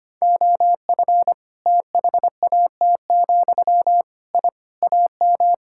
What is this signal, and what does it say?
Morse code, heard as a keyed tone, about 25 wpm, 700 Hz. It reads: OF THAT, I AM
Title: Sentences from Top 1000 Words - Review Part 5 - 25wpm